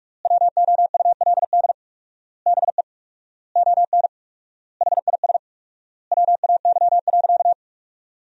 Morse code, heard as a keyed tone, about 41 wpm, 700 Hz. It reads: WOULD BE ON HIS WAY.
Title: 4x Word-Spacing - Sentences from Top 300 Words - 45wpm